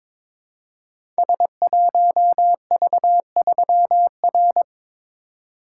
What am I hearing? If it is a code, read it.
Morse code, heard as a keyed tone, about 22 wpm, 700 Hz. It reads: S1V3R
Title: Code Groups - 5 Letter:Numbers - 22wpm